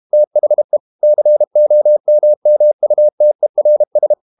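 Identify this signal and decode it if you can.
Morse code, heard as a keyed tone, about 32 wpm, 600 Hz. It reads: THE COMMUTERS
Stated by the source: Morse code practice at 32 wpm from CW med Gnister, reading Ken Follett's novel Never in English.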